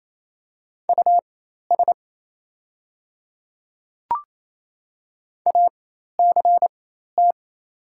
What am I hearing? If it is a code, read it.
Morse code, heard as a keyed tone, about 27 wpm, spaced as for 13 wpm, 700 Hz. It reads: US ACT
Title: 4x Character Spacing - Top 300 Words - 28wpm